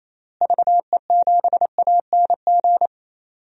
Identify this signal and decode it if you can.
Morse code, heard as a keyed tone, about 28 wpm, 700 Hz. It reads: VE7ANG
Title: DXCC-100 Call Signs - 6 Characters - Repeated 3x - 28wpm